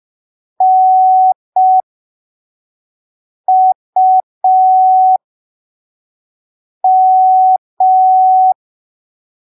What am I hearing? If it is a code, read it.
Morse code, heard as a keyed tone, about 5 wpm, spaced as for 3 wpm, 750 Hz. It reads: NUM
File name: arquivos / texto / texto 11 05ppm